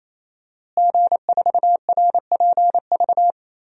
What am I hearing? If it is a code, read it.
Morse code, heard as a keyed tone, about 28 wpm, 700 Hz. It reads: G4RPV
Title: DXCC-100 Call Signs - 5 Characters - Repeated 3x - 28wpm